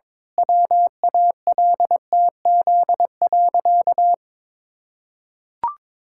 Morse code, heard as a keyed tone, about 22 wpm, 700 Hz. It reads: WALTZ.
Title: Pangram Challenge v2 - 22wpm